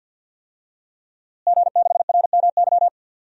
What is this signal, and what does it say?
Morse code, heard as a keyed tone, about 47 wpm, 700 Hz. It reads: G6RMY
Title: DXCC-100 Call Signs - 5 Characters - Repeated 2x - 50wpm